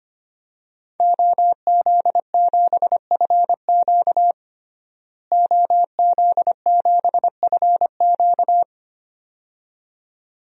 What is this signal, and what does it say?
Morse code, heard as a keyed tone, about 25 wpm, 700 Hz. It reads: OZ7FQ OZ7FQ
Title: DXCC-100 Call Signs - 5 Characters - Repeated 2x - 25wpm